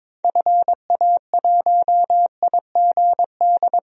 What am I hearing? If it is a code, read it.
Morse code, heard as a keyed tone, about 21 wpm, 700 Hz. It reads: FA1IGD
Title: DXCC-100 Call Signs - 6 Characters - 22wpm